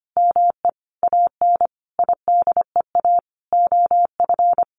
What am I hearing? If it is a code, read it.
Morse code, heard as a keyed tone, about 25 wpm, 700 Hz. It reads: ME AN IDEA OF
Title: Sentences from Top 700 Words 25wpm